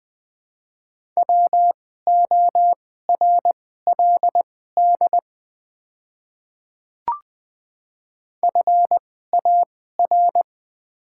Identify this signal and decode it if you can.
Morse code, heard as a keyed tone, about 20 wpm, spaced as for 15 wpm, 690 Hz. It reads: WORLD FAR
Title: 2x Character Spacing - Top 500 Words - 20wpm